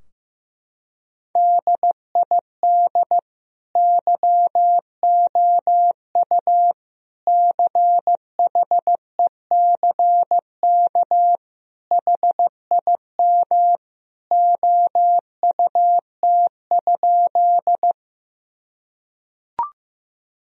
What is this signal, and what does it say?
Morse code, heard as a keyed tone, about 15 wpm, 700 Hz. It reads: DID YOU CHECK HIM OUT?
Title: Sentences from Top 1000 Words - Review Part 3 - 15wpm